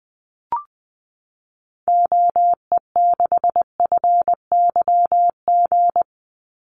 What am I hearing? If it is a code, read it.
Morse code, heard as a keyed tone, about 20 wpm, 700 Hz. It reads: OE6FYG
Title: Call Signs Repeated 3x - 20wpm